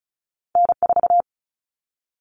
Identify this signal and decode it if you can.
Morse code, heard as a keyed tone, about 36 wpm, 700 Hz. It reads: N4
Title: Common Cut Numbers - 2-Digit 35wpm